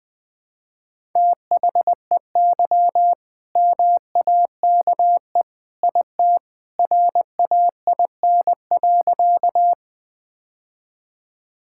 Morse code, heard as a keyed tone, about 20 wpm, 700 Hz. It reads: THEY MAKE IT RAIN.
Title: All-Age Appropriate Jokes - 20wpm